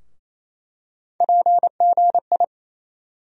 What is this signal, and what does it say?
Morse code, heard as a keyed tone, about 27 wpm, 700 Hz. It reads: PGI
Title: USA Call Sign Postfixes - 3 Letters 28wpm